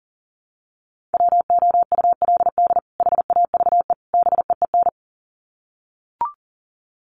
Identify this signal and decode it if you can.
Morse code, heard as a keyed tone, about 40 wpm, 700 Hz. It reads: WOULD HAVE BEEN
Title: Sets of 3 Words 40wpm